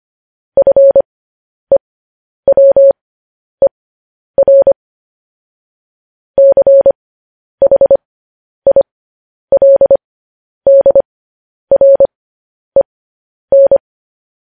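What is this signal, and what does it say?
Morse code, heard as a keyed tone, about 25 wpm, spaced as for 10 wpm, 550 Hz. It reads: FEWER CHILDREN
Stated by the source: Morse code news headlines practice, at 10 WPM.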